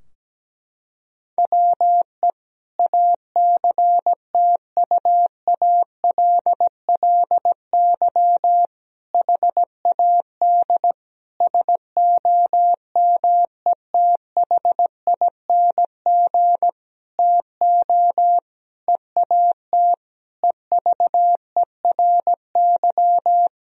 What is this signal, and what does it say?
Morse code, heard as a keyed tone, about 17 wpm, 700 Hz. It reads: WE ACTUALLY HAD SOMETHING TO EAT EVERY
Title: Sentences from Top 1000 Words - Review Part 4 - 17wpm